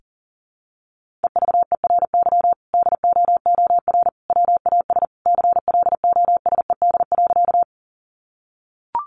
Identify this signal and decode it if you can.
Morse code, heard as a keyed tone, about 40 wpm, 700 Hz. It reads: EVERY DOOR WAS CLOSED.
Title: Sentences from Top 1400 Words 40wpm